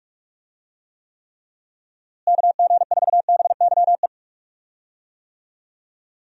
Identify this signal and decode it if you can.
Morse code, heard as a keyed tone, about 42 wpm, 700 Hz. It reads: KG4BYE
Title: USA Call Signs - 6 Characters - Repeated 3x - 45wpm